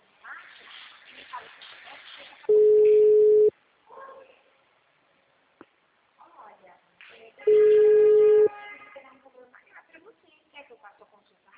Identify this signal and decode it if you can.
Ringback tone (Europe, 425 Hz), with speech.
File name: out- -2002- - - .12186